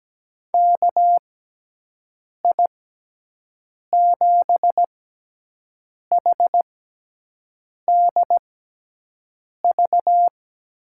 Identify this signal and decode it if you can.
Morse code, heard as a keyed tone, about 17 wpm, spaced as for 6 wpm, 700 Hz. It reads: KI7HDV
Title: USA Call Signs - 6x Character Spacing - 17wpm